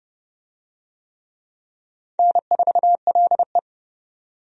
Morse code, heard as a keyed tone, about 29 wpm, 700 Hz. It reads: N4LE
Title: USA Call Signs - 4 Characters - 30wpm